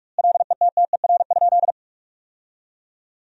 Morse code, heard as a keyed tone, about 47 wpm, 700 Hz. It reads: LETTER?
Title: 2x Word-Spacing - Sentences from Top 300 Words - 45wpm